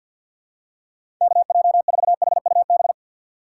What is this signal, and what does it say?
Morse code, heard as a keyed tone, about 50 wpm, 700 Hz. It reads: KJ4HUB
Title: USA Call Signs - 6 Characters - 6x Character Spacing - 50wpm